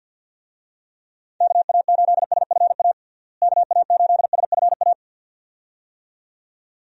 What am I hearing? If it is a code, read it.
Morse code, heard as a keyed tone, about 53 wpm, 700 Hz. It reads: KA8SFA KA8SFA
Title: USA Call Signs - 6 Characters - Repeated 2x - 50wpm